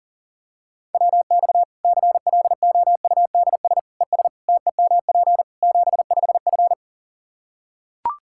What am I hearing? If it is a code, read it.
Morse code, heard as a keyed tone, about 40 wpm, 700 Hz. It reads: WX CLOUDS ES TEMP 75F
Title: K1USN SST QSO Practice 40wpm